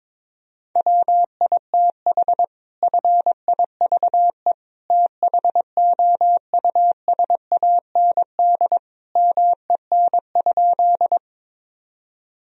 Morse code, heard as a keyed tone, about 21 wpm, 700 Hz. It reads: WITH FIVE THOUSAND MEN?
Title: Sentences from Top 1000 Words - Review Part 3 - 22wpm